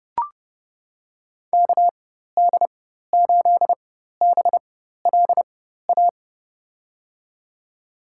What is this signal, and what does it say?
Morse code, heard as a keyed tone, about 30 wpm, spaced as for 14 wpm, 700 Hz. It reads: KD8BLA
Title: USA Call Signs - 6 Characters - 4x Character Spacing - 30wpm